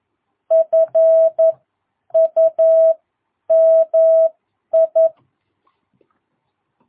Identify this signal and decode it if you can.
Morse code, heard as a keyed tone, about 11 wpm, spaced as for 9 wpm, 660 Hz. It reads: FUMI
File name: jh8cbh / 2022cw